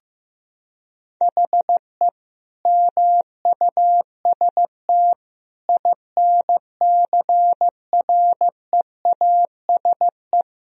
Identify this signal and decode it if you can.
Morse code, heard as a keyed tone, about 15 wpm, 700 Hz. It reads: HE MUST INCREASE
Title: Sentences from Top 1000 Words - Review Part 4 - 15wpm